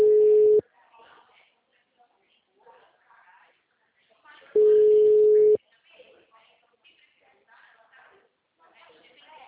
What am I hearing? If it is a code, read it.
Ringback tone (Europe, 425 Hz), with speech.
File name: out- -2012- - - .23220